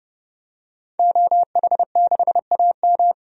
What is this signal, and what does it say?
Morse code, heard as a keyed tone, about 32 wpm, 700 Hz. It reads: OH6AM
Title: DXCC-100 Call Signs - 5 Characters - 30wpm